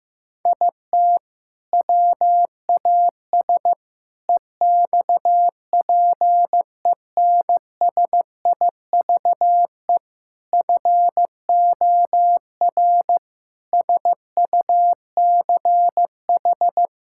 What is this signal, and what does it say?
Morse code, heard as a keyed tone, about 15 wpm, 700 Hz. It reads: IT WAS EXPENSIVE FOR SUCH
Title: Sentences from Top 1000 Words - Review Part 6 - 15wpm